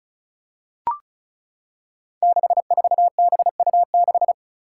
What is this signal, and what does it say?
Morse code, heard as a keyed tone, about 36 wpm, 700 Hz. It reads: 64BU6
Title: Five Challenging Letter-Number Combinations 35wpm